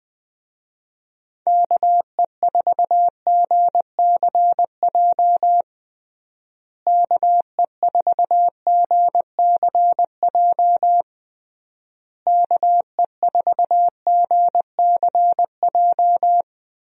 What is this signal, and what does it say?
Morse code, heard as a keyed tone, about 21 wpm, 700 Hz. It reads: KE4GCJ KE4GCJ KE4GCJ
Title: USA Call Signs - 6 Characters - Repeated 3x - 20wpm